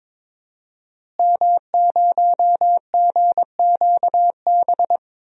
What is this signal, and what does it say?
Morse code, heard as a keyed tone, about 22 wpm, 700 Hz. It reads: M0GQB
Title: DXCC-100 Call Signs - 5 Characters - Repeated 3x - 22wpm